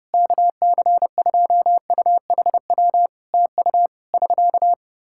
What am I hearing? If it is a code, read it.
Morse code, heard as a keyed tone, about 28 wpm, 700 Hz. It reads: KC2UHW TU <SK>
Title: Practice QSO - Exchange 4 - Ending - 30wpm